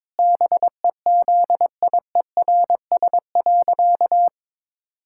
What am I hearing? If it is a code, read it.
Morse code, heard as a keyed tone, about 22 wpm, 700 Hz. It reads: BEZIERS.
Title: Pangram Challenge v2 - 22wpm